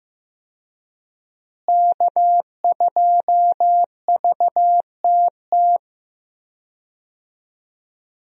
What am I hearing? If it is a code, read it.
Morse code, heard as a keyed tone, about 15 wpm, 700 Hz. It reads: K2VTT
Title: USA Call Signs - 5 Characters - Repeated 2x - 15wpm